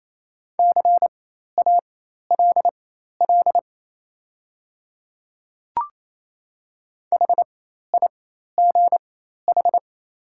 Morse code, heard as a keyed tone, about 28 wpm, spaced as for 13 wpm, 690 Hz. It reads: CALL HIGH
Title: 4x Character Spacing - Top 200 Words - 28wpm